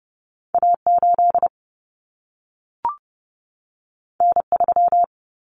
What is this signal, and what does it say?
Morse code, heard as a keyed tone, about 30 wpm, 700 Hz. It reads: A8 N3
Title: Common Cut Numbers - 2-Digit 30wpm